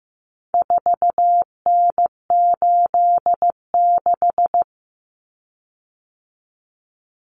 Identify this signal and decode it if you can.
Morse code, heard as a keyed tone, about 15 wpm, 700 Hz. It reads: 4N86
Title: Common Cut Numbers - 4-Digit 15wpm